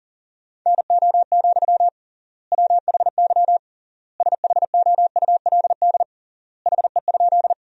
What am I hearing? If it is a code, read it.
Morse code, heard as a keyed tone, about 40 wpm, 700 Hz. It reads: NO, WHY SHOULD HE?
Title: 3x Word-Spacing - Sentences from Top 300 Words - 40wpm